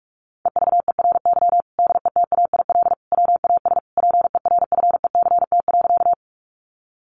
Morse code, heard as a keyed tone, about 44 wpm, 700 Hz. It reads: EVERY DETAIL WAS PERFECT.
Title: Sentences from Top 1500 Words 45wpm